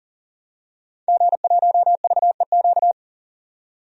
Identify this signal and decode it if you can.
Morse code, heard as a keyed tone, about 40 wpm, 700 Hz. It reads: G1VEQ
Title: DXCC-100 Call Signs - 5 Characters - 40wpm